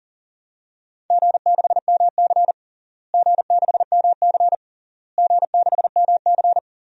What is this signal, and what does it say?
Morse code, heard as a keyed tone, about 37 wpm, 700 Hz. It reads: G6MC G6MC G6MC
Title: DXCC-100 Call Signs - 4 Characters - Repeated 3x - 40wpm